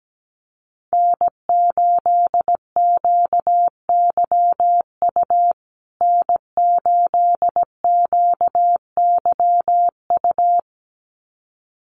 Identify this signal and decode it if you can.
Morse code, heard as a keyed tone, about 17 wpm, 700 Hz. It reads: N8QYU N8QYU
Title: USA Call Signs Repeated 2x 17wpm